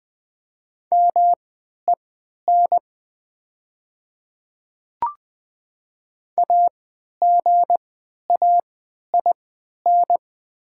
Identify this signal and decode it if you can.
Morse code, heard as a keyed tone, about 20 wpm, spaced as for 11 wpm, 700 Hz. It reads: MEN AGAIN
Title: 3x Character Spacing - Top 300 Words - 20wpm